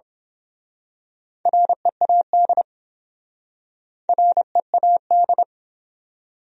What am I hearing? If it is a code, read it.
Morse code, heard as a keyed tone, about 26 wpm, 700 Hz. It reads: READ READ
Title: Speed-Racing Top 300 Words 22wpm